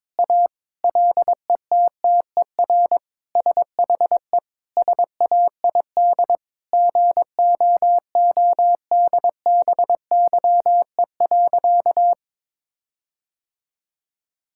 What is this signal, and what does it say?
Morse code, heard as a keyed tone, about 21 wpm, 700 Hz. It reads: A LETTER SHE SAID GOODBYE.
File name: practice / 22wpm / Sentences with 4-Word Phrases - 22wpm